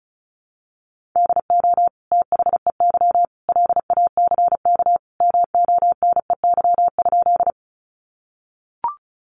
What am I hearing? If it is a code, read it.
Morse code, heard as a keyed tone, about 34 wpm, 700 Hz. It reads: DO THEY LACK MONEY?
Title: Sentences from Top 1100 Words 35wpm